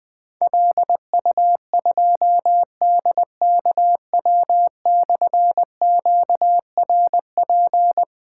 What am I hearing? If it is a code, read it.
Morse code, heard as a keyed tone, about 20 wpm, 700 Hz. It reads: LU2DKW/QRP
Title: Difficult Call Signs 20wpm